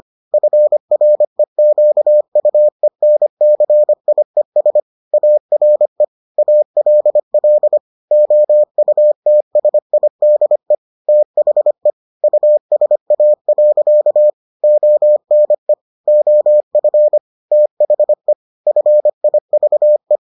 Morse code, heard as a keyed tone, about 25 wpm, 600 Hz. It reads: FREQUENCIES ARE ALL OUTSIDE THE USA. ONE OF THE FIVE